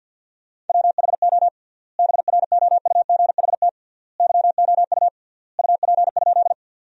Morse code, heard as a keyed tone, about 50 wpm, 700 Hz. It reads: WHO BROUGHT YOU UP?
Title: 3x Word-Spacing - Sentences from Top 500 Words - 50wpm